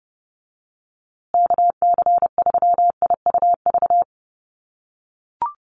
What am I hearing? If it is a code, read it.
Morse code, heard as a keyed tone, about 30 wpm, 700 Hz. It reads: KC3IUV
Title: USA Call Signs Repeated 2x 30wpm